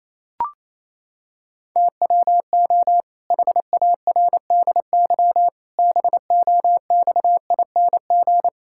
Morse code, heard as a keyed tone, about 27 wpm, 700 Hz. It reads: TWO HARDY BOXING
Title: Pangram Challenge v2 - 28wpm